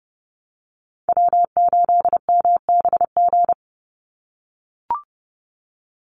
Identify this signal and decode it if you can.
Morse code, heard as a keyed tone, about 30 wpm, 700 Hz. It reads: W8MBG
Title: USA Call Signs 30wpm